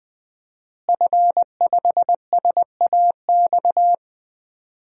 Morse code, heard as a keyed tone, about 20 wpm, 700 Hz. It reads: F5SAX
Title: DXCC-100 Call Signs - 5 Characters - Repeated 3x - 20wpm